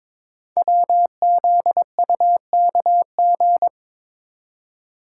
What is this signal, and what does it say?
Morse code, heard as a keyed tone, about 22 wpm, 700 Hz. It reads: WZUKG
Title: Code Groups - 5 Letters - 22wpm